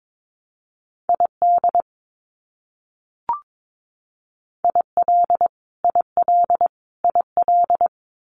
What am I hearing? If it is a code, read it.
Morse code, heard as a keyed tone, about 22 wpm, 700 Hz. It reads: ID IL IL IL
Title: US States Repeated 3x 22wpm